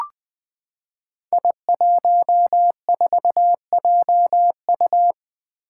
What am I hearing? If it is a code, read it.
Morse code, heard as a keyed tone, about 20 wpm, 700 Hz. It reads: I14JU
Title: Five Challenging Letter-Number Combinations 20wpm